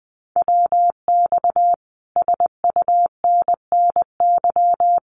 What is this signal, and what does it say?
Morse code, heard as a keyed tone, about 20 wpm, 700 Hz. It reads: WX SUNNY
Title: Common QSO Phrases Encore 20wpm